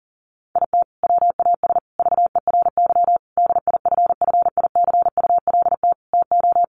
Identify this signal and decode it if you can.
Morse code, heard as a keyed tone, about 40 wpm, 700 Hz. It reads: IT WAS VERY DIFFICULT TO
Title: Sentences from Top 1100 Words 40wpm